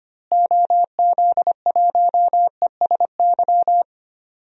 Morse code, heard as a keyed tone, about 25 wpm, 700 Hz. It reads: OZ1ESY
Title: DXCC-100 Call Signs - 6 Characters - Repeated 2x - 25wpm